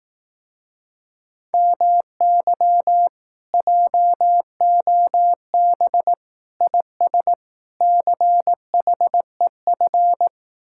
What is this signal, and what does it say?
Morse code, heard as a keyed tone, about 18 wpm, 700 Hz. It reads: MY JOB IS CHEF
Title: Speed-Racing Common QSO Phrases Encore - 15wpm